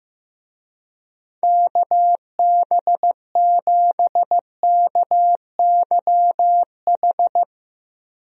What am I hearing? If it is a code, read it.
Morse code, heard as a keyed tone, about 15 wpm, 700 Hz. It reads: KB7KYH
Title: USA Call Signs - 6 Characters - 15wpm